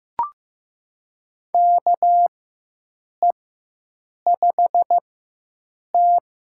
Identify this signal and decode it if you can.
Morse code, heard as a keyed tone, about 15 wpm, spaced as for 7 wpm, 700 Hz. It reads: KE5T
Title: USA Call Signs - 4x Character Spacing - 15wpm